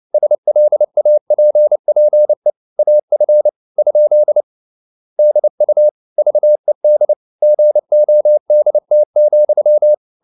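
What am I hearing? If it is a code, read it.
Morse code, heard as a keyed tone, about 29 wpm, 600 Hz. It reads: SLAPPE AF? DU VED GODT,
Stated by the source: Fast (QRQ) Morse at 29 wpm, from CW med Gnister.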